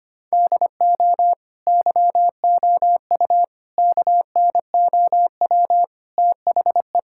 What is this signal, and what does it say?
Morse code, heard as a keyed tone, about 25 wpm, 700 Hz. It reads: DO YOU KNOW THE
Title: Sentences from Top 1000 Words - Review Part 6 - 25wpm